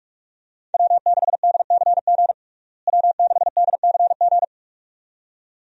Morse code, heard as a keyed tone, about 46 wpm, 700 Hz. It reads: W6DCG W6DCG
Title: USA Call Signs - 5 Characters - Repeated 2x - 45wpm